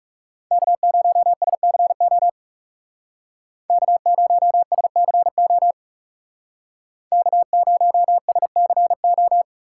Morse code, heard as a keyed tone, about 35 wpm, 700 Hz. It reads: K0SCO K0SCO K0SCO
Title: Speed Racing - USA 1x3 Call Signs 30wpm